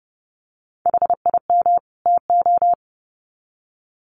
Morse code, heard as a keyed tone, about 30 wpm, 700 Hz. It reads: HIM TO
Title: Sets of 2 Words 30wpm